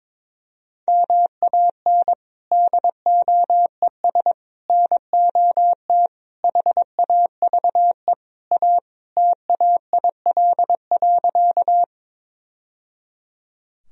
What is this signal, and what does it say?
Morse code, heard as a keyed tone, about 21 wpm, 700 Hz. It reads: MAN DOES NOT HAVE A TAIL.
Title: Sentences from Top 1000 Words - Review Part 3 - 22wpm